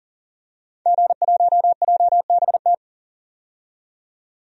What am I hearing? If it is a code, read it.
Morse code, heard as a keyed tone, about 40 wpm, 700 Hz. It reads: G1JBT
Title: DXCC-100 Call Signs - 5 Characters - 40wpm